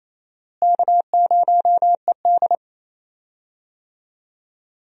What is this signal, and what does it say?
Morse code, heard as a keyed tone, about 27 wpm, 700 Hz. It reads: K0ED
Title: USA Call Signs - 4 Characters - Repeated 3x - 28wpm